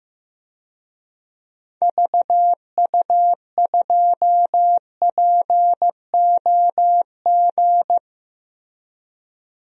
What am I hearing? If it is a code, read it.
Morse code, heard as a keyed tone, about 15 wpm, 700 Hz. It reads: VU2POG
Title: DXCC-100 Call Signs - 6 Characters - Repeated 3x - 15wpm